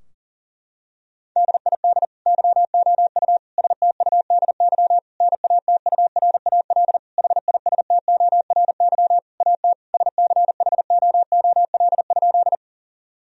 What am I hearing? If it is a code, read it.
Morse code, heard as a keyed tone, about 40 wpm, 700 Hz. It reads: DID YOU STUDY NATURAL HISTORY AT SCHOOL?
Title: Sentences from Top 1000 Words - Review Part 4 - 40wpm